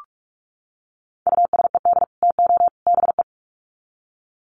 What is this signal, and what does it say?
Morse code, heard as a keyed tone, about 44 wpm, 700 Hz. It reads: USED TO BE
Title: Sets of 3 Words 45wpm